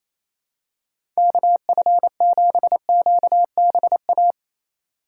Morse code, heard as a keyed tone, about 28 wpm, 700 Hz. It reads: KF7QBA
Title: USA Call Signs - 6 Characters - 3x Character Spacing - 28wpm